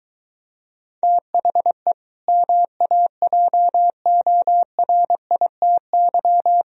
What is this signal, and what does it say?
Morse code, heard as a keyed tone, about 23 wpm, 700 Hz. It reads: THE MAJORITY